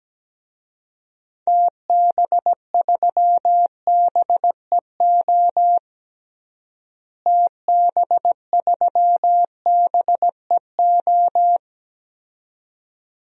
Morse code, heard as a keyed tone, about 17 wpm, 700 Hz. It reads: TB3BEO TB3BEO
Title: DXCC-100 Call Signs - 6 Characters - Repeated 2x - 17wpm